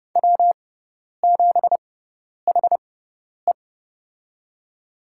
Morse code, heard as a keyed tone, about 31 wpm, spaced as for 10 wpm, 700 Hz. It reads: W7HE
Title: USA Call Signs - 4 Characters - 6x Character Spacing - 30wpm